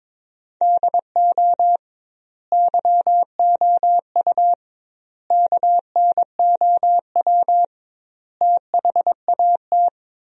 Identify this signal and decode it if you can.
Morse code, heard as a keyed tone, about 22 wpm, 700 Hz. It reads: DO YOU KNOW THAT
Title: 2x Word-Spacing - Sentences from Top 400 Words - 22wpm